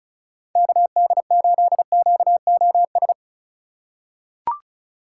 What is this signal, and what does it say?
Morse code, heard as a keyed tone, about 36 wpm, 700 Hz. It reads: KD8QOS
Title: USA Call Signs - 6 Characters - 4x Character Spacing - 35wpm